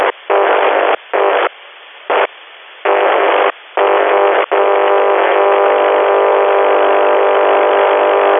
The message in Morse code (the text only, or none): none